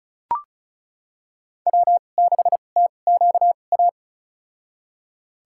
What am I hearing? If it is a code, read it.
Morse code, heard as a keyed tone, about 36 wpm, spaced as for 25 wpm, 700 Hz. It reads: W6TQA
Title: USA Call Signs - 5 Characters - 2x Character Spacing - 35wpm